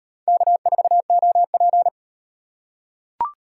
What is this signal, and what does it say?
Morse code, heard as a keyed tone, about 36 wpm, 700 Hz. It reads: K4OP